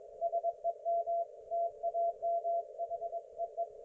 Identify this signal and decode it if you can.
Morse code, heard as a keyed tone, about 22 wpm, 680 Hz. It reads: SEM TAMHEE
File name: Archiv / CUC216 / ok1fhi cul216